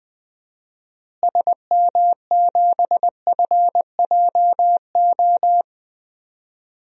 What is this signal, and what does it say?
Morse code, heard as a keyed tone, about 20 wpm, 700 Hz. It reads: SM7FJO
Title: DXCC-100 Call Signs - 6 Characters - Repeated 2x - 20wpm